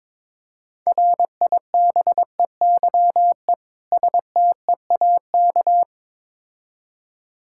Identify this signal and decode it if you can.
Morse code, heard as a keyed tone, about 23 wpm, 690 Hz. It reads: RIBEYE STEAK
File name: Difficult Words with Hints - 22wpm